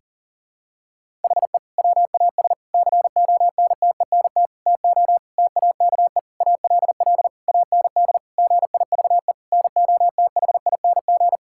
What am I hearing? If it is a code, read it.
Morse code, heard as a keyed tone, about 40 wpm, 700 Hz. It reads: HE WAS CONTENT TO TAKE ALL AND GIVE NOTHING
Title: Sentences from Top 1000 Words - Review Part 6 - 40wpm